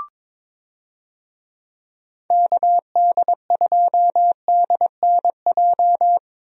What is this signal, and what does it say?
Morse code, heard as a keyed tone, about 22 wpm, 700 Hz. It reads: KD2DNJ
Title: USA Call Signs - 6 Characters - Repeated 2x - 22wpm